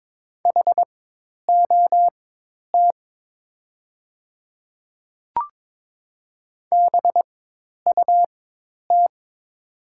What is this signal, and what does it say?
Morse code, heard as a keyed tone, about 22 wpm, spaced as for 10 wpm, 700 Hz. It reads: HOT BUT
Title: 4x Character Spacing - Top 100 Words - 22wpm